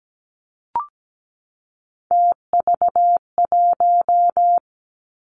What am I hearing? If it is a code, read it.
Morse code, heard as a keyed tone, about 17 wpm, 700 Hz. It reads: TV1
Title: Three Challenging Letter-Number Combinations 17wpm